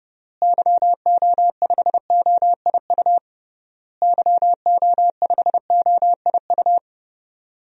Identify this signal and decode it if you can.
Morse code, heard as a keyed tone, about 30 wpm, 700 Hz. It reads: YO5OIU YO5OIU
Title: DXCC-100 Call Signs - 6 Characters - Repeated 3x - 30wpm